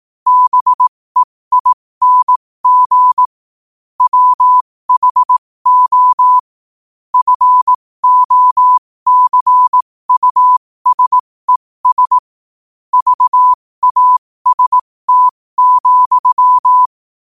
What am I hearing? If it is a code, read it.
Morse code, heard as a keyed tone, about 18 wpm, spaced as for 15 wpm, 1000 Hz. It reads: BEING WHO FOCUSES VAST,
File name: qotdfiles / 15wpm